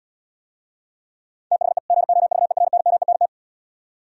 Morse code, heard as a keyed tone, about 78 wpm, 700 Hz. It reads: THE GOVERNMENT